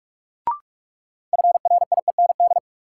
Morse code, heard as a keyed tone, about 46 wpm, 700 Hz. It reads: FRIEND
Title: Sentences from Top 300 Words - New Word Drill - 45wpm